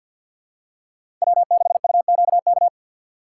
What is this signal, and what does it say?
Morse code, heard as a keyed tone, about 47 wpm, 700 Hz. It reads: W6UQK
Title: USA Call Signs - 4x Character Spacing - 50wpm